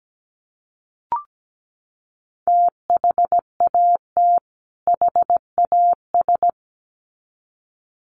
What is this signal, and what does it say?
Morse code, heard as a keyed tone, about 17 wpm, 700 Hz. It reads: THAT HAS
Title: Sets of 2 Words 17wpm